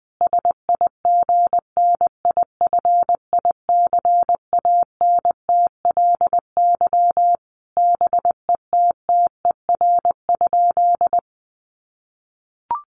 Morse code, heard as a keyed tone, about 20 wpm, 700 Hz. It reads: SIGNIFICANTLY BETTER?
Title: Sentences from Top 1500 Words 20wpm